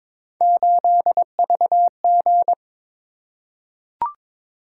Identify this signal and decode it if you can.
Morse code, heard as a keyed tone, about 22 wpm, 700 Hz. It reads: 8VG
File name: practice / 22wpm / Three Letter-Number Combinations - 22wpm